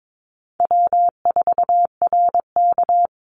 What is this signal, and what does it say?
Morse code, heard as a keyed tone, about 22 wpm, 700 Hz. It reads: W4RK
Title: Call Signs Repeated 2x - 22wpm